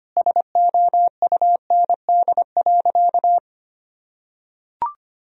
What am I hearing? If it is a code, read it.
Morse code, heard as a keyed tone, about 25 wpm, 700 Hz. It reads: SOUND.
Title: Sentences from Top 1000 Words - Review Part 1 - 25wpm